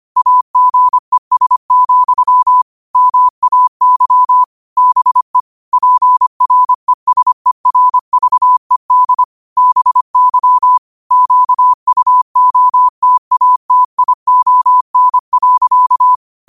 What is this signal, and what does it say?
Morse code, heard as a keyed tone, about 25 wpm, 1000 Hz. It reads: AGES, MAY BE PRESERVED BY QUOTATION.
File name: qotdfiles / 25wpm